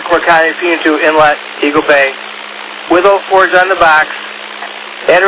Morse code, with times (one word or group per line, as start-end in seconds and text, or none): none